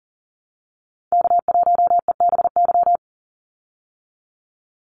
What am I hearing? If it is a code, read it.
Morse code, heard as a keyed tone, about 40 wpm, 700 Hz. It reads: K1EBY
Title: Call Signs Repeated 2x - 40wpm